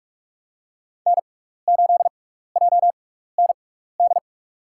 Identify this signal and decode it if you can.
Morse code, heard as a keyed tone, about 46 wpm, spaced as for 15 wpm, 690 Hz. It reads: N8JND
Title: USA Call Signs - 5 Characters - 6x Character Spacing - 45wpm